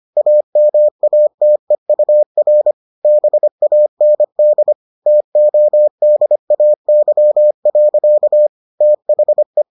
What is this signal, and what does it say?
Morse code, heard as a keyed tone, about 25 wpm, 600 Hz. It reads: AMATEUR BAND TODAY. THE